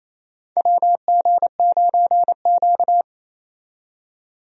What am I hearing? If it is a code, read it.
Morse code, heard as a keyed tone, about 28 wpm, 700 Hz. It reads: WG9Q
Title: USA Call Signs - 4 Characters - 28wpm